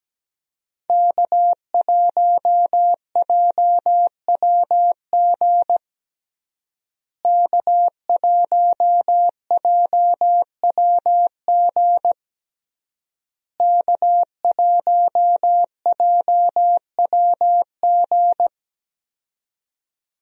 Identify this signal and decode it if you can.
Morse code, heard as a keyed tone, about 17 wpm, 700 Hz. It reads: K1JWG K1JWG K1JWG
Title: USA Call Signs - 5 Characters - Repeated 3x - 17wpm